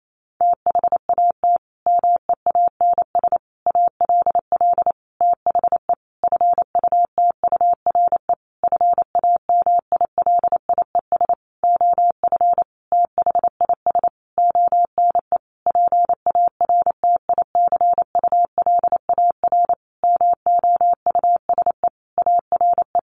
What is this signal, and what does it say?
Morse code, heard as a keyed tone, about 28 wpm, 700 Hz. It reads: THAT MEANS ALL THE FUTURE FAMILIES OF THIS ONE PARTICULAR MOUSE ARE
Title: A Sound Of Thunder 28wpm